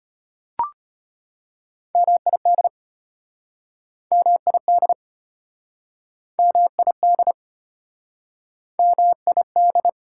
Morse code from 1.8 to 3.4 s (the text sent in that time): MID